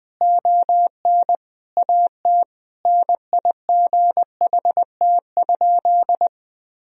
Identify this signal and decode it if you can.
Morse code, heard as a keyed tone, about 20 wpm, 700 Hz. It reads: ON AT NIGHT?
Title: Sentences from Top 1000 Words - Review Part 4 - 20wpm